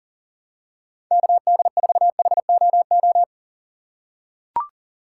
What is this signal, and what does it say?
Morse code, heard as a keyed tone, about 40 wpm, 700 Hz. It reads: KD4HOO
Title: USA Call Signs - 6 Characters - 2x Character Spacing - 40wpm